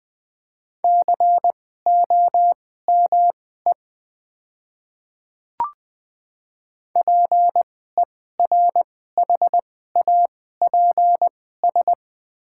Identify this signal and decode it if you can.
Morse code, heard as a keyed tone, about 20 wpm, spaced as for 14 wpm, 700 Hz. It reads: COME PERHAPS
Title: 2x Character Spacing - Top 500 Words - 20wpm